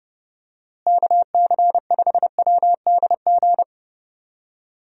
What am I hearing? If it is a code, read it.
Morse code, heard as a keyed tone, about 30 wpm, 700 Hz. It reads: KC5WDG
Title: USA Call Signs - 6 Characters - Repeated 3x - 30wpm